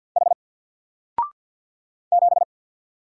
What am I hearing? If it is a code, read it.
Morse code, heard as a keyed tone, about 48 wpm, 700 Hz. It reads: H 7
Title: Single Letter-Number 50wpm